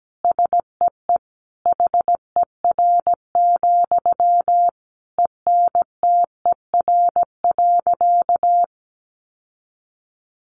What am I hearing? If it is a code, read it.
Morse code, heard as a keyed tone, about 17 wpm, 700 Hz. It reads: SEE HER, ENTER.
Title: Sentences from Top 1300 Words 17wpm